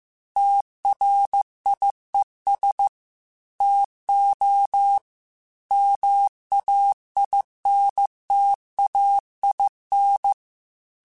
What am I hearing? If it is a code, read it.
Morse code, heard as a keyed tone, about 15 wpm, 800 Hz. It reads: TRIES TO MAINTAIN
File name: Technical Subjects / Technical Subjects-mp3s / XL @13